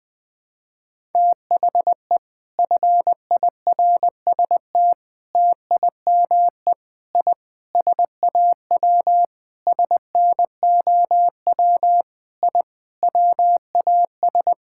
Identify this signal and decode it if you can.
Morse code, heard as a keyed tone, about 20 wpm, 700 Hz. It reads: THE FIRST TIME I SAW SNOW I WAS
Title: Sentences with 4-Word Phrases - 20wpm